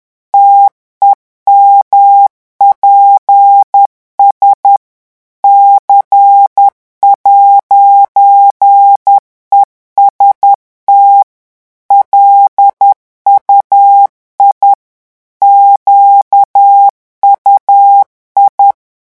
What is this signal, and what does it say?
Morse code, heard as a keyed tone, about 10 wpm, 800 Hz. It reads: TEMPS C'EST LUI QUI
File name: morse mp3 / revision francais 10 mots minute